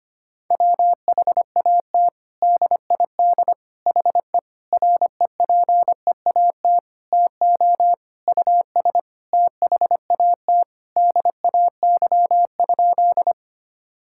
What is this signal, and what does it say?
Morse code, heard as a keyed tone, about 25 wpm, 700 Hz. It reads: WHAT DID HE REPEAT TO US THAT DAY?
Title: Sentences from Top 1600 Words 25wpm